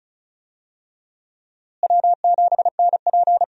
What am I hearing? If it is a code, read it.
Morse code, heard as a keyed tone, about 36 wpm, 700 Hz. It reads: W7NP
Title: USA Call Signs - 4 Characters - Repeated 3x - 35wpm